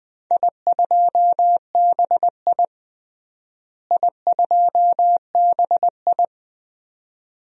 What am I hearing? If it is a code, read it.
Morse code, heard as a keyed tone, about 21 wpm, 700 Hz. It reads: I2BI I2BI
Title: DXCC-100 Call Signs - 4 Characters - Repeated 2x - 20wpm